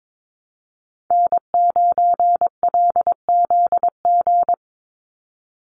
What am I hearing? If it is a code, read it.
Morse code, heard as a keyed tone, about 22 wpm, 700 Hz. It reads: N9LZG
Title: USA Call Signs 22wpm